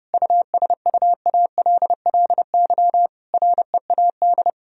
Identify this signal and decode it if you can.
Morse code, heard as a keyed tone, about 30 wpm, 700 Hz. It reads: USUALLY READ
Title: Sentences from Top 1000 Words - Review Part 3 - 30wpm